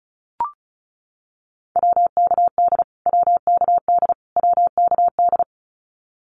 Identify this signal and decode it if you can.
Morse code, heard as a keyed tone, about 34 wpm, 700 Hz. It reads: WKD WKD WKD
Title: Abbreviations Repeated 3x - 35wpm